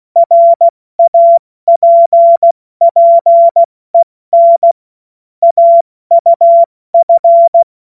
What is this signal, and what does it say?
Morse code, heard as a keyed tone, about 16 wpm, 680 Hz. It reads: RAPPEN AUF
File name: Morsekurs / Witze / Witz 15. November 2022 70 F80